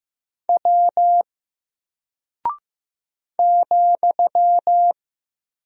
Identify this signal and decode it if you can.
Morse code, heard as a keyed tone, about 15 wpm, 700 Hz. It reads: W ,
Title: Single Character (Letter:Number:Punctuation:Prosign) Alt 15wpm